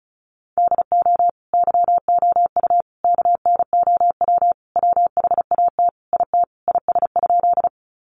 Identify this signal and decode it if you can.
Morse code, heard as a keyed tone, about 34 wpm, 700 Hz. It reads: DO YOU KNOW WHAT IT IS?
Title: Sentences from Top 100 Words 35wpm